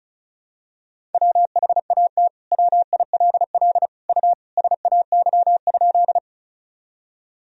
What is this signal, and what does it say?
Morse code, heard as a keyed tone, about 34 wpm, 700 Hz. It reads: WHAT WILL U SAY?
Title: Sentences from Top 200 Words and Abbreviations 35wpm